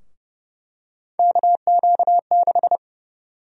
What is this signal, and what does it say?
Morse code, heard as a keyed tone, about 30 wpm, 700 Hz. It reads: KQ6
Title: USA Call Sign Prefixes - Repeated 2x - 30wpm